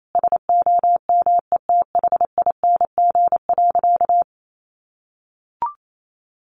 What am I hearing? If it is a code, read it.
Morse code, heard as a keyed tone, about 28 wpm, 700 Hz. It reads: SOMETHING.
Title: Sentences from Top 1200 Words 28wpm